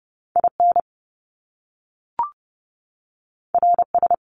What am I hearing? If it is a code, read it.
Morse code, heard as a keyed tone, about 30 wpm, 700 Hz. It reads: IN RS
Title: Top Two Letter Combinations in Words 30wpm